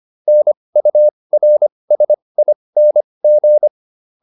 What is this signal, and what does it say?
Morse code, heard as a keyed tone, about 25 wpm, spaced as for 20 wpm, 600 Hz. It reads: NURSING